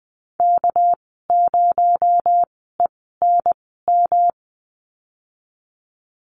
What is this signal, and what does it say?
Morse code, heard as a keyed tone, about 19 wpm, spaced as for 14 wpm, 700 Hz. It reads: K0ENM
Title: 245 Lesson 39g - Letter z - Call Signs 20wpm